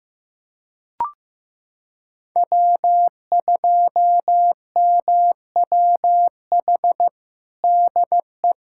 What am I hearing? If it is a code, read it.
Morse code, heard as a keyed tone, about 15 wpm, 700 Hz. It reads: W2MWH DE
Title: USA Callsign DE Callsign - 5 Characters - 15wpm